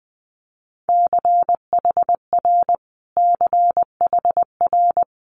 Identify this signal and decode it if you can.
Morse code, heard as a keyed tone, about 20 wpm, 700 Hz. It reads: CHR CHR
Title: Multi-letter Phonemes Repeated 3x 20wpm